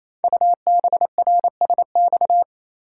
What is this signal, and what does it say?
Morse code, heard as a keyed tone, about 27 wpm, 700 Hz. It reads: UBRSX
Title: Code Groups - 5 Letters - 28wpm